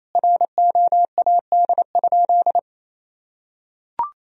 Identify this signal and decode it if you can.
Morse code, heard as a keyed tone, about 28 wpm, 700 Hz. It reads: ROAD?
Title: Sentences from Top 1000 Words - Review Part 2 - 28wpm